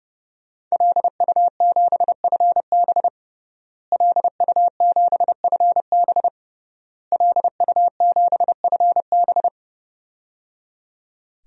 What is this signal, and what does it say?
Morse code, heard as a keyed tone, about 30 wpm, 700 Hz. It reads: LU7FB LU7FB LU7FB
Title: DXCC-100 Call Signs - 5 Characters - Repeated 3x - 30wpm